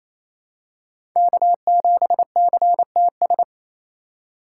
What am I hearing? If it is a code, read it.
Morse code, heard as a keyed tone, about 28 wpm, 700 Hz. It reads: K7CTS
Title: USA Call Signs - 5 Characters - 2x Character Spacing - 28wpm